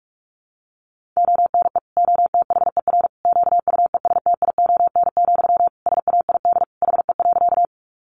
Morse code, heard as a keyed tone, about 44 wpm, 700 Hz. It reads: ONE OTHER QUESTION, SAID HE.
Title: Sentences from Top 400 Words 45wpm